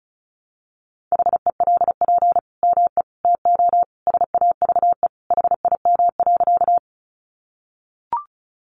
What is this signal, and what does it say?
Morse code, heard as a keyed tone, about 34 wpm, 700 Hz. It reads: HELP ME TO SAVE HIM.
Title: Sentences from Top 1100 Words 35wpm